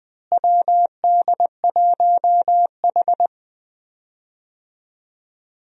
Morse code, heard as a keyed tone, about 20 wpm, 700 Hz. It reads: WD1H
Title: USA Call Signs - 4 Characters - Repeated 2x - 20wpm